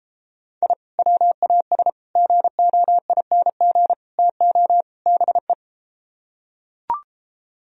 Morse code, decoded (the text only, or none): I WAS GOING TO BE